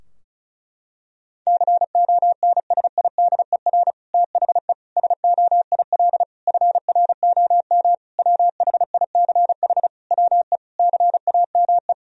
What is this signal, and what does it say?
Morse code, heard as a keyed tone, about 36 wpm, 700 Hz. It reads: CONSIDER THE SOIL FROM WHICH WE CAME
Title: Sentences from Top 1000 Words - Review Part 4 - 35wpm